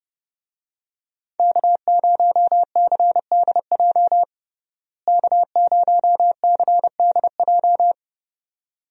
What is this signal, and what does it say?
Morse code, heard as a keyed tone, about 29 wpm, 700 Hz. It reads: K0CDJ K0CDJ
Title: USA Call Signs - 5 Characters - Repeated 2x - 30wpm